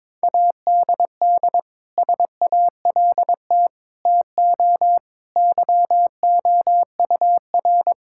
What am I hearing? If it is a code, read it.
Morse code, heard as a keyed tone, about 22 wpm, 700 Hz. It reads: ADD SALT TO YOUR
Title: Sentences from Top 1000 Words - Review Part 5 - 22wpm